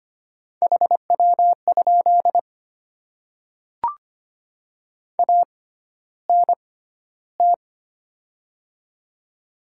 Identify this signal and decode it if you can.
Morse code, heard as a keyed tone, about 25 wpm, spaced as for 13 wpm, 700 Hz. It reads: HW? ANT
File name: Top 20 QSO Elements - 6x Character Spacing - 25wpm